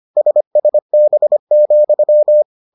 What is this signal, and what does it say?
Morse code, heard as a keyed tone, about 25 wpm, 600 Hz. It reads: SSB,